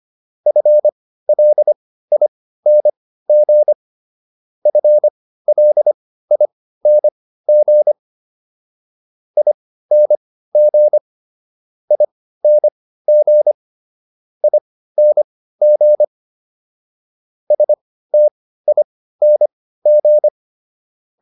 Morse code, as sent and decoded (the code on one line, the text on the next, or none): ..-. .-.. .. -. --. / ..-. .-.. .. -. --. / .. -. --. / .. -. --. / .. -. --. / ... - .. -. --.
FLING FLING ING ING ING STING